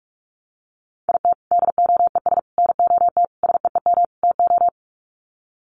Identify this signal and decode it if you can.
Morse code, heard as a keyed tone, about 44 wpm, 700 Hz. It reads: IT DOES NOT SEEM TO
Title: Sets of 5 Words Encore2 45wpm